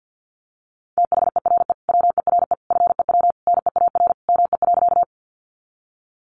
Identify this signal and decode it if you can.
Morse code, heard as a keyed tone, about 50 wpm, 700 Hz. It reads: THERE WERE FEW NEAR ME.
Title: Sentences from Top 300 Words 50wpm